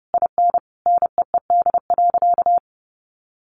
Morse code, heard as a keyed tone, about 30 wpm, 700 Hz. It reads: IN NEED.
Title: Sentences from Top 1200 Words 30wpm